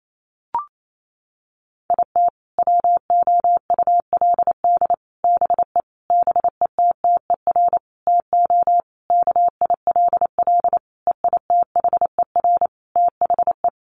Morse code, heard as a keyed tone, about 28 wpm, 700 Hz. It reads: IT WOULD BE BETTER TO KILL EITHER THE
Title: Sentences from Top 1500 Words 28wpm